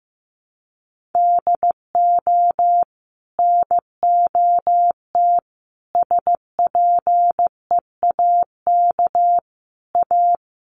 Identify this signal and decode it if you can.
Morse code, heard as a keyed tone, about 15 wpm, 700 Hz. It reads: DO NOT SPEAK A
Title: Sentences from Top 1100 Words 15wpm